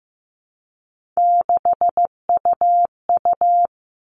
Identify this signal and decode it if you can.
Morse code, heard as a keyed tone, about 15 wpm, 700 Hz. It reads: 6UU
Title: All Cut Numbers - 3-Digit 15wpm